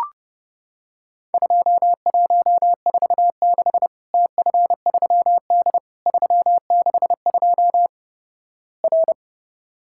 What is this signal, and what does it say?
Morse code, heard as a keyed tone, about 30 wpm, 700 Hz. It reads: 2146 TF3D 362 R
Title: Worked All Europe DX Contest - QTC Exchange - 30wpm